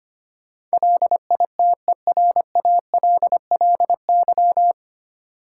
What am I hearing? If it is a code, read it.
Morse code, heard as a keyed tone, about 25 wpm, 700 Hz. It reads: LITERALLY
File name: Nine Letter Words - 25wpm